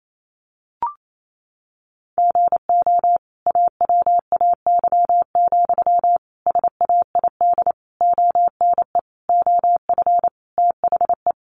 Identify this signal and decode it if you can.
Morse code, heard as a keyed tone, about 28 wpm, 700 Hz. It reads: GO AWAY, SAID ONE OF THE
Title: Sentences from Top 1500 Words 28wpm